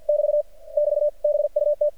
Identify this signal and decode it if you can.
Morse code, heard as a keyed tone, about 50 wpm, 590 Hz. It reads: <BT> <BT> BUT
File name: archived / CRSnet